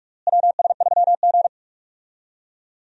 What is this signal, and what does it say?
Morse code, heard as a keyed tone, about 41 wpm, 700 Hz. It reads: WS3G
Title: USA Call Signs - 4 Characters - Repeated 3x - 45wpm